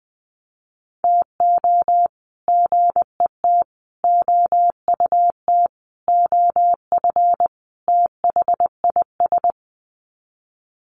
Morse code, heard as a keyed tone, about 20 wpm, 700 Hz. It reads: TO GET OUT OF THIS
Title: Sets of 5 Words Encore2 20wpm